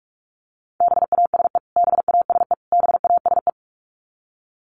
Morse code, heard as a keyed tone, about 44 wpm, 700 Hz. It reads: BASE BASE BASE
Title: Top 500 Words Repeated 45wpm